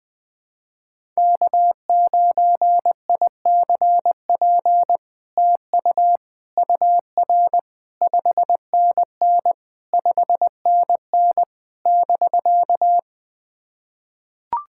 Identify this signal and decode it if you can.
Morse code, heard as a keyed tone, about 21 wpm, 700 Hz. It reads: K9ICP TU UR 5NN 5NN <BK>
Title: POTA QSO Practice - 20wpm